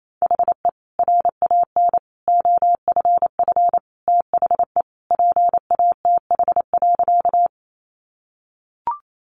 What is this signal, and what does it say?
Morse code, heard as a keyed tone, about 28 wpm, 700 Hz. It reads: HE RAN OFF THE PATH.
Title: A Sound Of Thunder 28wpm